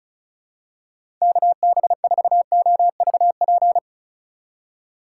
Morse code, heard as a keyed tone, about 36 wpm, 700 Hz. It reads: KB4OVP
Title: USA Call Signs - 6 Characters - 4x Character Spacing - 35wpm